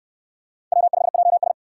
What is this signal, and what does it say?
Morse code, heard as a keyed tone, about 62 wpm, 700 Hz. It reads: W5JS